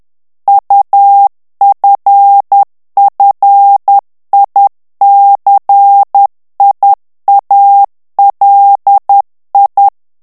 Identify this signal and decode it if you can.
Morse code, heard as a keyed tone, about 10 wpm, 800 Hz. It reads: UFFICIALI
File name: ITALIEN texte clair 2 vitesse 10 mots minute